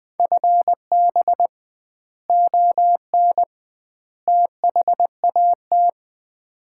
Morse code, heard as a keyed tone, about 20 wpm, 700 Hz. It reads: FB ON THAT
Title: Common QSO Phrases - 2x Word-Spacing - 20wpm